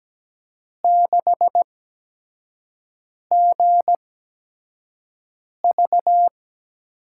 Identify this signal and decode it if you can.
Morse code, heard as a keyed tone, about 17 wpm, 700 Hz. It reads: 6 G V
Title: USA Call Signs - 5 Characters - 8x Character Spacing - 17wpm